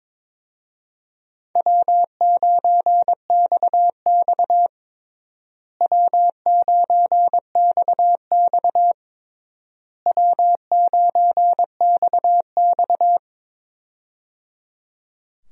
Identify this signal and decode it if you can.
Morse code, heard as a keyed tone, about 22 wpm, 700 Hz. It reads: W9XX W9XX W9XX
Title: USA Call Signs - 4 Characters - Repeated 3x - 22wpm